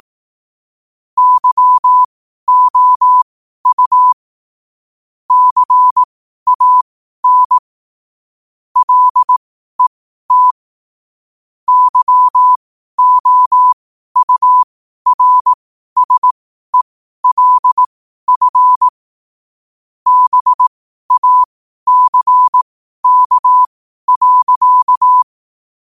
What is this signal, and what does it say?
Morse code, heard as a keyed tone, about 18 wpm, spaced as for 12 wpm, 1000 Hz. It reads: YOU CAN LET YOURSELF BACK.
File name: qotdfiles / 12wpm